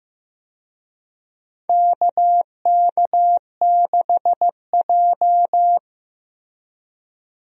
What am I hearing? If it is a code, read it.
Morse code, heard as a keyed tone, about 15 wpm, 700 Hz. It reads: KK6J
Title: USA Call Signs - 4 Characters - 15wpm